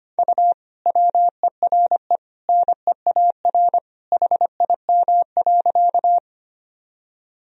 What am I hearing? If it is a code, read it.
Morse code, heard as a keyed tone, about 25 wpm, 700 Hz. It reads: U WERE NEAR HIM.
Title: Sentences from Top 200 Words and Abbreviations 25wpm